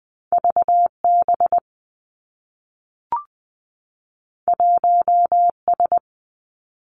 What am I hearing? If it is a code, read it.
Morse code, heard as a keyed tone, about 20 wpm, 700 Hz. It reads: V B 1 S
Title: Two Challenging Letter-Number Combinations 20wpm